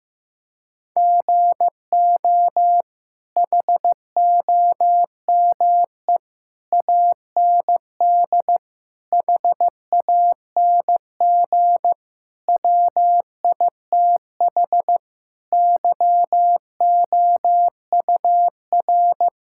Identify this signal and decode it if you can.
Morse code, heard as a keyed tone, about 16 wpm, 700 Hz. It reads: GO HOME AND HANG WITH YOUR
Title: Sentences from Top 1600 Words 15wpm